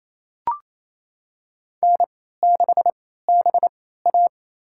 Morse code, heard as a keyed tone, about 28 wpm, spaced as for 16 wpm, 700 Hz. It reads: N6BA
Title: USA Call Signs - 4 Characters - 3x Character Spacing - 28wpm